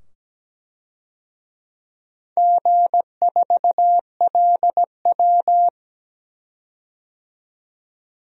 Morse code, heard as a keyed tone, about 17 wpm, 700 Hz. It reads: G4LW
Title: DXCC-100 Call Signs - 4 Characters - 17wpm